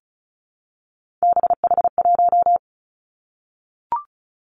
Morse code, heard as a keyed tone, about 36 wpm, 700 Hz. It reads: BH1
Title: Three Challenging Letter-Number Combinations 35wpm